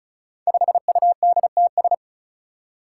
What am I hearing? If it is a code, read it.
Morse code, heard as a keyed tone, about 36 wpm, 700 Hz. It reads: 5UDTS
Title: Five Challenging Letter-Number Combinations 35wpm